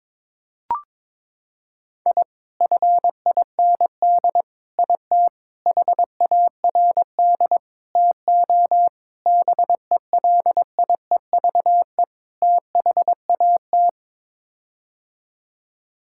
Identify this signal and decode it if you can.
Morse code, heard as a keyed tone, about 22 wpm, 700 Hz. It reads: I FIND IT HARD TO BELIEVE THAT
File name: practice / 22wpm / Sets of 7 Words - 22wpm